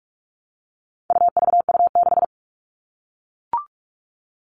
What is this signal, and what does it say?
Morse code, heard as a keyed tone, about 44 wpm, 700 Hz. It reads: UVU6
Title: All Cut Numbers - 4-Digit 45wpm